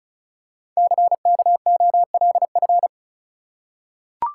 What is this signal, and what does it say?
Morse code, heard as a keyed tone, about 34 wpm, 700 Hz. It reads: CKOLF
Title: Code Groups - 5 Letters - 35wpm